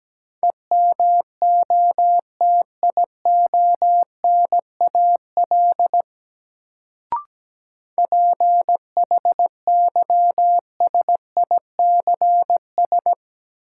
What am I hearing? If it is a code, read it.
Morse code, heard as a keyed tone, about 17 wpm, 700 Hz. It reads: EMOTIONAL PHYSICS
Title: Sentences from Top 1000 Words - Review Part 5 - New Word Drill - 17wpm